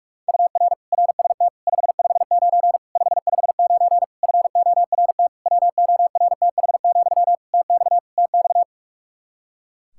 Morse code, heard as a keyed tone, about 42 wpm, 700 Hz. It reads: UR RST 559 559 FORT WORTH, TX TX
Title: Practice QSO - Exchange 1 - Beginning - 45wpm